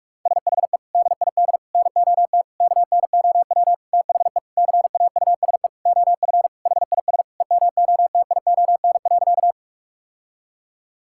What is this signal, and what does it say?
Morse code, heard as a keyed tone, about 41 wpm, 700 Hz. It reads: SHE DID NOT KNOW THE CAUSE OF HIS EMOTION.
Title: Sentences from Top 1600 Words 45wpm